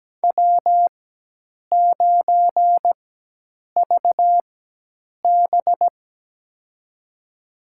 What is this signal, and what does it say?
Morse code, heard as a keyed tone, about 17 wpm, spaced as for 8 wpm, 700 Hz. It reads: W9VB
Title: USA Call Signs - 4x Character Spacing - 17wpm